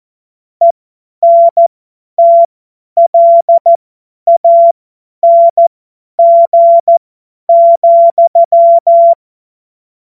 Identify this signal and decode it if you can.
Morse code, heard as a keyed tone, about 14 wpm, spaced as for 10 wpm, 680 Hz. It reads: ENTLANG,
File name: Morsekurs / Witze / Witz 16. Dezember 2025 50 F70